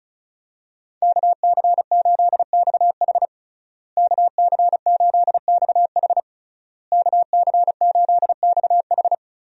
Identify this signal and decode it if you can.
Morse code, heard as a keyed tone, about 34 wpm, 700 Hz. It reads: KC8XH KC8XH KC8XH
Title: USA Call Signs - 5 Characters - Repeated 3x - 35wpm